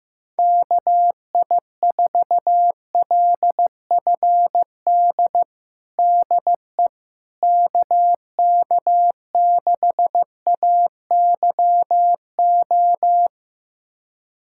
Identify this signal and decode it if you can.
Morse code, heard as a keyed tone, about 15 wpm, 700 Hz. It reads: KI4LFD DE KK6AYO
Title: USA Callsign DE Callsign 15wpm